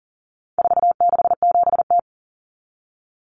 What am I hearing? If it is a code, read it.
Morse code, heard as a keyed tone, about 40 wpm, 700 Hz. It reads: 467T
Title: Common Cut Numbers - 4-Digit 40wpm